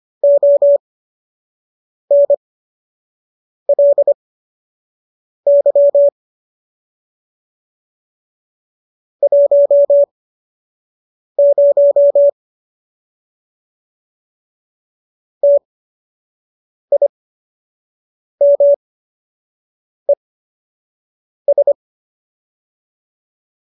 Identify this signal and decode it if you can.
Morse code, heard as a keyed tone, about 25 wpm, spaced as for 6 wpm, 570 Hz. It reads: ONLY 10 TIMES